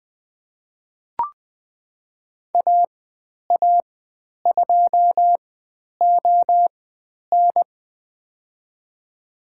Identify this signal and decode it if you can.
Morse code, heard as a keyed tone, about 20 wpm, spaced as for 10 wpm, 700 Hz. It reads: AA2ON
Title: 102 Lesson 17g - Letter u - Call Signs 20wpm